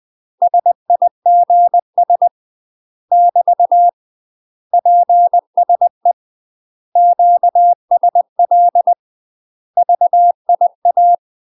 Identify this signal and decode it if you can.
Morse code, heard as a keyed tone, about 20 wpm, 700 Hz. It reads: SIGS <BT> PSE QSL VIA
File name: UK MASTER QSOs TRAINING / 20WPM / UK MASTER QSOs 0030